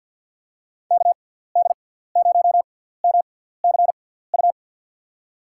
Morse code, heard as a keyed tone, about 52 wpm, spaced as for 17 wpm, 700 Hz. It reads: KD0MCU
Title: USA Call Signs - 6 Characters - 6x Character Spacing - 50wpm